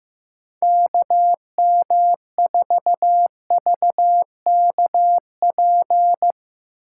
Morse code, heard as a keyed tone, about 16 wpm, 700 Hz. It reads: KM4VKP
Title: Speed Racing - USA 2x3 Call Signs 15wpm